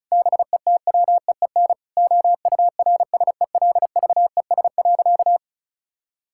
Morse code, heard as a keyed tone, about 34 wpm, 700 Hz. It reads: BETWEEN OURSELVES.
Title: Sentences from Top 1600 Words 35wpm